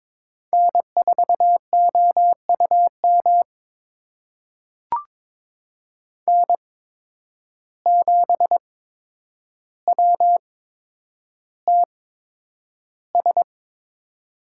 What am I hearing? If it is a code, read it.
Morse code, heard as a keyed tone, about 22 wpm, spaced as for 9 wpm, 700 Hz. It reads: N4OUM N7WTS
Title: USA Call Signs - 5 Characters - 8x Character Spacing - 22wpm